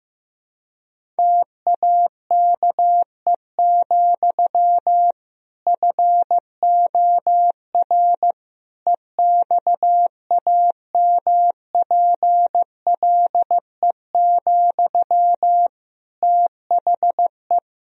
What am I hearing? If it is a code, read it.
Morse code, heard as a keyed tone, about 15 wpm, 700 Hz. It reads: TAKE, FOR EXAMPLE, THE
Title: Sentences from Top 1000 Words - Review Part 1 - 15wpm